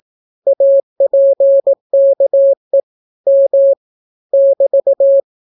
Morse code, heard as a keyed tone, about 18 wpm, 550 Hz. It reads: APKE M <BT>